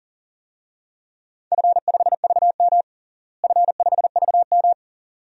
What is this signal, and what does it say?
Morse code, heard as a keyed tone, about 40 wpm, 700 Hz. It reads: F5VM F5VM
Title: DXCC-100 Call Signs - 4 Characters - Repeated 2x - 40wpm